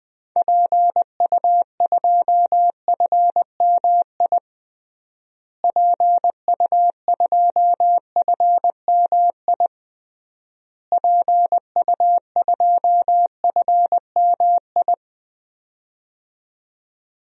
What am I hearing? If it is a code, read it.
Morse code, heard as a keyed tone, about 20 wpm, 700 Hz. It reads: PU2FMI PU2FMI PU2FMI
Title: DXCC-100 Call Signs - 6 Characters - Repeated 3x - 20wpm